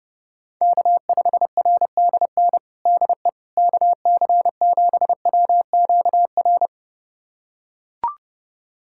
Morse code, heard as a keyed tone, about 30 wpm, 700 Hz. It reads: K5RDN DE KC7WQR
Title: USA Callsign DE Callsign 30wpm